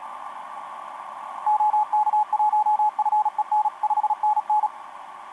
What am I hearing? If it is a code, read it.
Morse code, heard as a keyed tone, about 37 wpm, 850 Hz, with radio noise. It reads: OK1FEN 5NN
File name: s21yz ba8ba etc 12m cw 22mar2011 0925